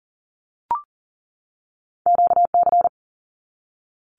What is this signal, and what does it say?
Morse code, heard as a keyed tone, about 40 wpm, 700 Hz. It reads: QC
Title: Canadian Provinces and Territories 40wpm